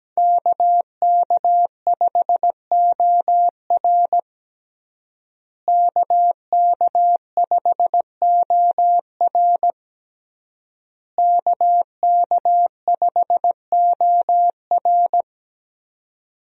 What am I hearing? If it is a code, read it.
Morse code, heard as a keyed tone, about 17 wpm, 700 Hz. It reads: KK5OR KK5OR KK5OR
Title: USA Call Signs - 5 Characters - Repeated 3x - 17wpm